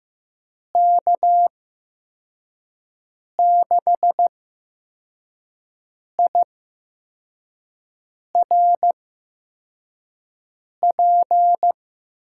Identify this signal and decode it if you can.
Morse code, heard as a keyed tone, about 15 wpm, spaced as for 4 wpm, 700 Hz. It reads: K6IRP
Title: USA Call Signs - 8x Character Spacing - 15wpm